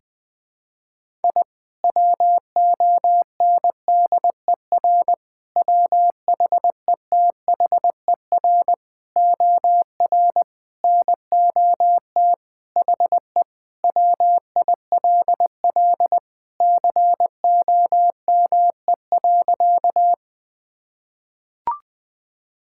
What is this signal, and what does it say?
Morse code, heard as a keyed tone, about 20 wpm, 700 Hz. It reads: I WONDER WHETHER OR NOT HE WILL COME.
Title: Sentences with 3-Word Phrases - 20wpm